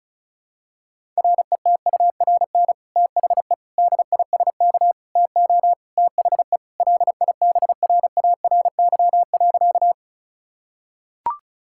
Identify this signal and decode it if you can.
Morse code, heard as a keyed tone, about 36 wpm, 700 Hz. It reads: RETURN THE DISK TO THE LIBRARY.
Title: Sentences from Top 1000 Words - Review Part 6 - 35wpm